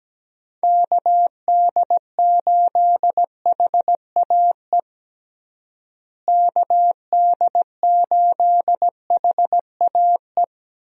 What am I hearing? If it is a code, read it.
Morse code, heard as a keyed tone, about 17 wpm, 700 Hz. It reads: KD8HAE KD8HAE
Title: USA Call Signs - 6 Characters - Repeated 3x - 17wpm